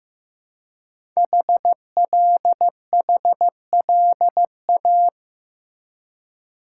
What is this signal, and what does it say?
Morse code, heard as a keyed tone, about 15 wpm, 700 Hz. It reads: HLHLA
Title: Code Groups - 5 Letters - 15wpm